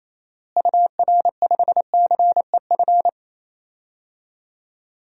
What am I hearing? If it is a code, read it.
Morse code, heard as a keyed tone, about 28 wpm, 700 Hz. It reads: UR5CEF
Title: DXCC-100 Call Signs - 6 Characters - Repeated 3x - 28wpm